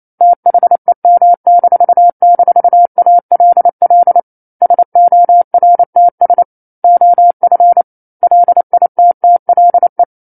Morse code, heard as a keyed tone, about 28 wpm, 700 Hz. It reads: THEM--ALL SORTS OF LITTLE
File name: 09-Chapter-09 28 WPM